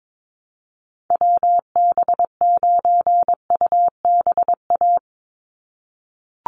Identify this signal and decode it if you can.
Morse code, heard as a keyed tone, about 22 wpm, 700 Hz. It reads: WB9UBA
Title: USA Call Signs 22wpm